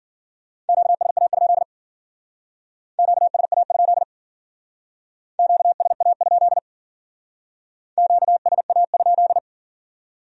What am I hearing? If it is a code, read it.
Morse code, heard as a keyed tone, about 48 wpm, 700 Hz. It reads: QSA? QSA? QSA? QSA?
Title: Speed-Racing Q-Codes - 40wpm